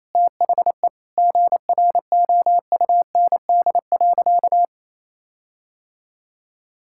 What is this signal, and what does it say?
Morse code, heard as a keyed tone, about 28 wpm, 700 Hz. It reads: THE GROUND.
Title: Sentences from Top 1600 Words 28wpm